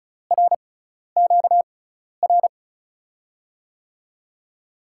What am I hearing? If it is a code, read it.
Morse code, heard as a keyed tone, about 36 wpm, spaced as for 12 wpm, 700 Hz. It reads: RQR
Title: USA Call Signs - 6 Characters - 6x Character Spacing - 35wpm